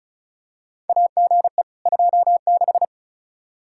Morse code, heard as a keyed tone, about 34 wpm, 700 Hz. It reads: AGE 26
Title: Practice QSO - Exchange 3 - More Info - 35wpm